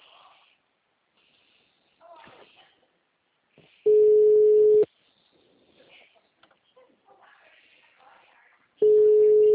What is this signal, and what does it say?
Ringback tone (Europe, 425 Hz), with speech.